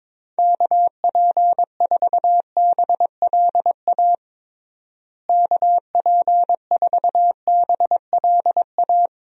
Morse code, heard as a keyed tone, about 22 wpm, 700 Hz. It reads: KP4BLA KP4BLA
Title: DXCC-100 Call Signs - 6 Characters - Repeated 3x - 22wpm